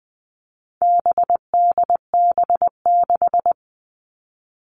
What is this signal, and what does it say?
Morse code, heard as a keyed tone, about 20 wpm, 700 Hz. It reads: BDB6
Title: All Cut Numbers - 4-Digit 20wpm